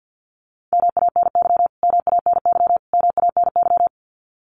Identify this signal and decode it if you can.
Morse code, heard as a keyed tone, about 50 wpm, 700 Hz. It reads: MANY MANY MANY
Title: Top 500 Words Repeated 50wpm